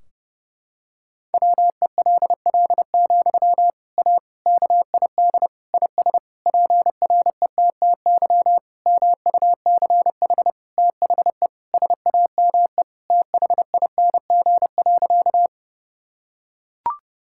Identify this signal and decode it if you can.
Morse code, heard as a keyed tone, about 30 wpm, 700 Hz. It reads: WELL, A KID IS PRETTY MUCH THE SAME THING.
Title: Sentences from Top 1600 Words 30wpm